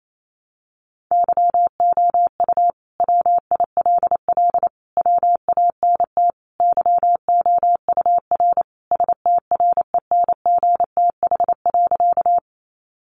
Morse code, heard as a keyed tone, about 28 wpm, 700 Hz. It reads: YOU WILL WANT YOUR STRENGTH.
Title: Sentences from Top 1300 Words 28wpm